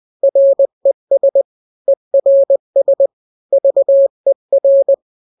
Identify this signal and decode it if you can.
Morse code, heard as a keyed tone, about 20 wpm, 550 Hz. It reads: RES ERS VER
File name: n-grams / 3-grams 20 18 WPM